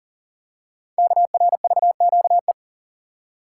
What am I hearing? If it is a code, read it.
Morse code, heard as a keyed tone, about 39 wpm, 700 Hz. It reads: KRVQE
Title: Code Groups - 5 Letters - 40wpm